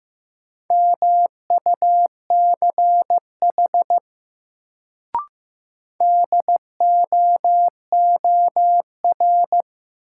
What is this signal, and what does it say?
Morse code, heard as a keyed tone, about 15 wpm, 700 Hz. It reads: MUCH DOOR
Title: Sets of 2 Words - New Word Drill - 15wpm